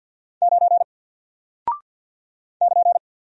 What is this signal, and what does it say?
Morse code, heard as a keyed tone, about 48 wpm, 700 Hz. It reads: 9 <KN>
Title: Single Character (Letter:Number:Punctuation:Prosign) Alt 50wpm